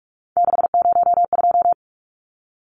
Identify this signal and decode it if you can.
Morse code, heard as a keyed tone, about 44 wpm, 700 Hz. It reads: 602
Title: Numbers - 3-Digit 45wpm